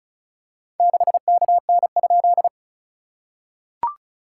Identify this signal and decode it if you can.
Morse code, heard as a keyed tone, about 36 wpm, 690 Hz. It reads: 6KN?
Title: SOTA QSO Practice 35wpm